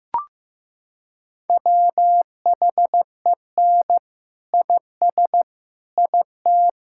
Morse code, heard as a keyed tone, about 15 wpm, 700 Hz. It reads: WHEN IS IT
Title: Sentences from Top 1000 Words - Review Part 6 - 15wpm